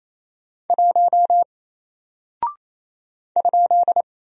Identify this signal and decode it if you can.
Morse code, heard as a keyed tone, about 27 wpm, 700 Hz. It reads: 1 ?
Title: Single Character (Letter:Number:Punctuation:Prosign) Alt 28wpm